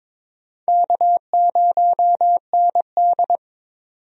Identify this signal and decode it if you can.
Morse code, heard as a keyed tone, about 22 wpm, 700 Hz. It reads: K0ND
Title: USA Call Signs - 4 Characters - 22wpm